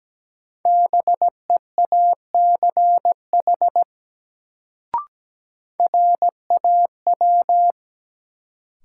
Morse code, heard as a keyed tone, about 17 wpm, 700 Hz. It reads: BEACH RAW
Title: Sentences from Top 1500 Words - New Word Drill - 17wpm